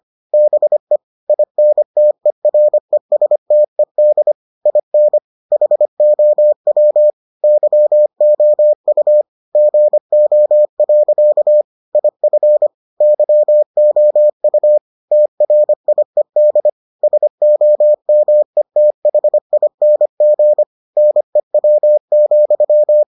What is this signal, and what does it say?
Morse code, heard as a keyed tone, about 25 wpm, 600 Hz. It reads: BE INTERESTED IN HOW YOU GO. IF YOU TRIED SOMETHING NEW,